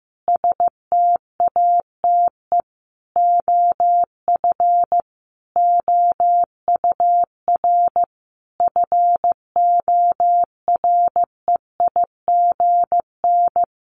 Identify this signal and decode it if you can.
Morse code, heard as a keyed tone, about 14 wpm, 700 Hz. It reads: STATE OF OUR FOREIGN
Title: Sentences from Top 1400 Words 15wpm